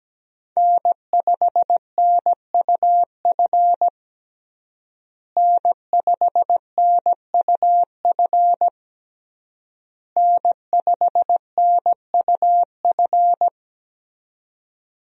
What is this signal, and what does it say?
Morse code, heard as a keyed tone, about 17 wpm, 700 Hz. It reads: N5NUF N5NUF N5NUF
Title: USA Call Signs - 5 Characters - Repeated 3x - 17wpm